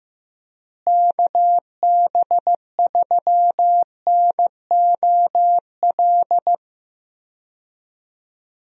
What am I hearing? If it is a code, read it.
Morse code, heard as a keyed tone, about 15 wpm, 700 Hz. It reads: KB3NOL
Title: USA Call Signs - 6 Characters - Repeated 2x - 15wpm